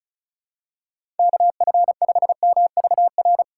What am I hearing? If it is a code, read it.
Morse code, heard as a keyed tone, about 34 wpm, 700 Hz. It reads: KF5MVR
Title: USA Call Signs - 3x Character Spacing - 35wpm